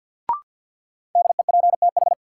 Morse code, heard as a keyed tone, about 53 wpm, 700 Hz. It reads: DEPTH
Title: Sentences from Top 1200 Words - New Word Drill - 50wpm